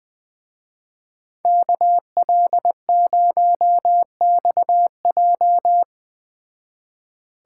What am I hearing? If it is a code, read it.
Morse code, heard as a keyed tone, about 20 wpm, 700 Hz. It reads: KL0XJ
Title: Speed Racing - USA 2x2 Call Signs 17wpm